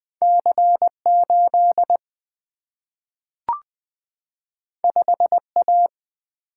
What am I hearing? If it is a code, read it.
Morse code, heard as a keyed tone, about 20 wpm, 700 Hz. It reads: C 8 5 A
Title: Single-Letter Single-Number 20wpm